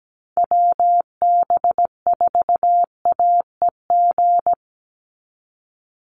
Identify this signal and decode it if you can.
Morse code, heard as a keyed tone, about 17 wpm, 700 Hz. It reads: WB4AEG
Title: Call Signs - 17wpm